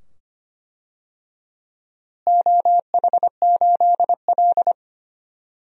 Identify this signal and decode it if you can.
Morse code, heard as a keyed tone, about 25 wpm, 700 Hz. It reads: OH8L
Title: DXCC-100 Call Signs - 4 Characters - Repeated 2x - 25wpm